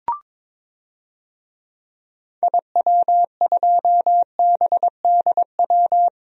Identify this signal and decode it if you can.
Morse code, heard as a keyed tone, about 22 wpm, 700 Hz. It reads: IW2BDW
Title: DXCC-100 Call Signs - 6 Characters - Repeated 3x - 22wpm